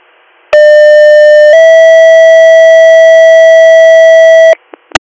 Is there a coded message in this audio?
Yes: two-tone page from 0.5 to 4.5 s, heard as a one-second tone, then a three-second tone, about 600 then 650 Hz.